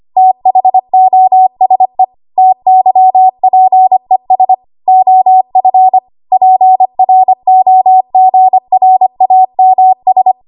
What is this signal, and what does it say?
Morse code, heard as a keyed tone, about 25 wpm, 750 Hz. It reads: THOSE TYPES OF PROGRAMS
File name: Archive / 25 WPM / 25WPM